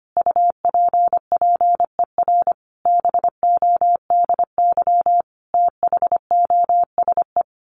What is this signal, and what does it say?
Morse code, heard as a keyed tone, about 25 wpm, 700 Hz. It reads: UPPER BODY THOSE
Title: A Sound Of Thunder 25wpm